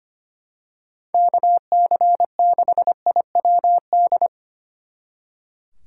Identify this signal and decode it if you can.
Morse code, heard as a keyed tone, about 26 wpm, 700 Hz. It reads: KC6IWD
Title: Speed Racing - USA 2x3 Call Signs 25wpm